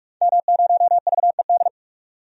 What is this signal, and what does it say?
Morse code, heard as a keyed tone, about 46 wpm, 700 Hz. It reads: M0VED
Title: DXCC-100 Call Signs - 5 Characters - 45wpm